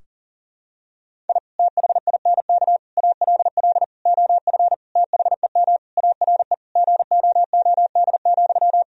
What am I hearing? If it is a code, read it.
Morse code, heard as a keyed tone, about 37 wpm, 700 Hz. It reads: I THINK ALL OF THEM ARE GOOD,
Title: Sentences from Top 1000 Words - Review Part 4 - 40wpm